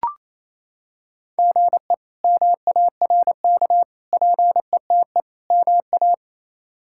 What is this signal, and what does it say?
Morse code, heard as a keyed tone, about 28 wpm, 700 Hz. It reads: GE MARK PETE MA
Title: K1USN SST QSO Practice - 28wpm